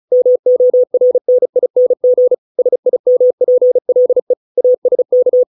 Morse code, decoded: MORNING SIMPLE ASK

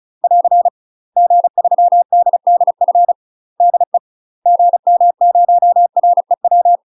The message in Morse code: <AR> G3DDF DE GM0REW

